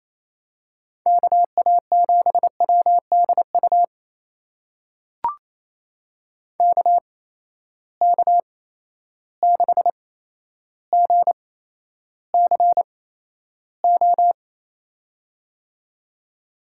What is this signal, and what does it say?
Morse code, heard as a keyed tone, about 28 wpm, spaced as for 12 wpm, 700 Hz. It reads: KA7WDU KK6GCO
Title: USA Call Signs - 6 Characters - 8x Character Spacing - 28wpm